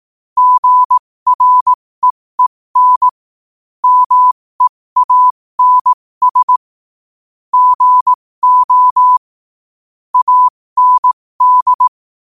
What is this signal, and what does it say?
Morse code, heard as a keyed tone, about 18 wpm, spaced as for 15 wpm, 1000 Hz. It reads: GREEN MEANS GO AND
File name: qotdfiles / 15wpm